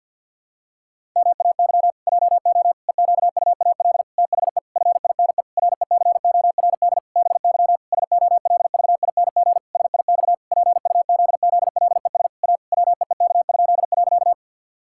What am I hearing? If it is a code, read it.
Morse code, heard as a keyed tone, about 50 wpm, 700 Hz. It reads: MAY JO EQUAL THE FINE RECORD BY SOLVING SIX PUZZLES A WEEK?.
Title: Pangram Challenge v2 - 50wpm